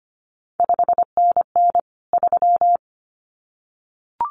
Morse code, heard as a keyed tone, about 25 wpm, 700 Hz. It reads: 5NN 3
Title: CQ WW DX Contest Exchange Encore - 25wpm